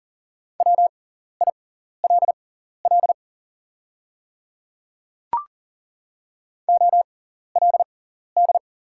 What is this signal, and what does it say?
Morse code, heard as a keyed tone, about 40 wpm, spaced as for 14 wpm, 700 Hz. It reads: WILL OLD
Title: 6x Character Spacing - Top 200 Words - 40wpm